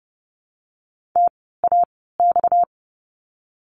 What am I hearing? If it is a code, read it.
Morse code, heard as a keyed tone, about 30 wpm, spaced as for 17 wpm, 700 Hz. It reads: TAX
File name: course / 30:10wpm / 230 Lesson 37f - Letter x - Words 30wpm